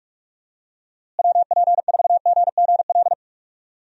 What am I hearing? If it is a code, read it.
Morse code, heard as a keyed tone, about 45 wpm, 700 Hz. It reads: WP4GGL
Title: USA Call Signs - 6 Characters - Repeated 3x - 45wpm